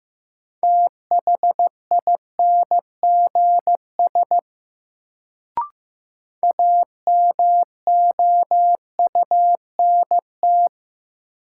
Morse code, heard as a keyed tone, about 15 wpm, 700 Hz. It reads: THINGS AMOUNT
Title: Sets of 3 Words - New Word Drill - 15wpm